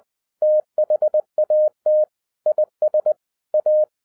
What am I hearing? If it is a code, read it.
Morse code, heard as a keyed tone, about 20 wpm, 610 Hz. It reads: THAT IS A
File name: 2011-07-mp3 / n08 A LOT HAS CHANGED 20wpm